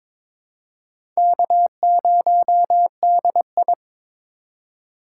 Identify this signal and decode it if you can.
Morse code, heard as a keyed tone, about 21 wpm, 700 Hz. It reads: K0DI
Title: USA Call Signs - 4 Characters - 4x Character Spacing - 22wpm